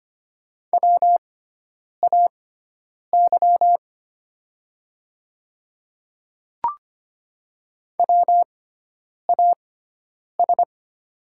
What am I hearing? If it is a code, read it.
Morse code, heard as a keyed tone, about 25 wpm, spaced as for 9 wpm, 700 Hz. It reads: WAY WAS
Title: 6x Character Spacing - Top 100 Words - 25wpm